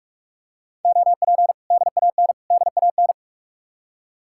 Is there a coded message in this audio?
Yes: Morse code from 0.8 to 3.1 s, heard as a keyed tone, about 45 wpm, 700 Hz. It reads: OP DAN DAN